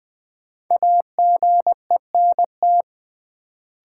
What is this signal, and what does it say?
Morse code, heard as a keyed tone, about 20 wpm, 700 Hz. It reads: AGENT
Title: Sentences from Top 1600 Words - New Word Drill - 20wpm